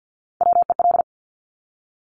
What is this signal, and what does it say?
Morse code, heard as a keyed tone, about 50 wpm, 700 Hz. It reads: WEL
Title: Top Three Letter Combinations in Words 50wpm